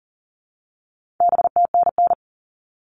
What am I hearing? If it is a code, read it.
Morse code, heard as a keyed tone, about 40 wpm, 700 Hz. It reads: BTNN
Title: All Cut Numbers - 4-Digit 40wpm